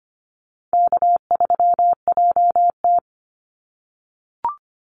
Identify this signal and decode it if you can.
Morse code, heard as a keyed tone, about 25 wpm, 700 Hz. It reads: K3JT
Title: Easy Call Signs - 25wpm